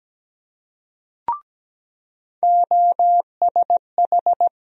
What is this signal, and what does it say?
Morse code, heard as a keyed tone, about 17 wpm, 700 Hz. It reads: OSH
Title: USA Call Sign Postfixes - 3 Letters 17wpm